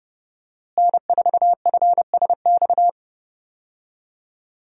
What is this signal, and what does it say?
Morse code, heard as a keyed tone, about 30 wpm, 700 Hz. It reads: N4FSX
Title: USA Call Signs - 5 Characters - Repeated 3x - 30wpm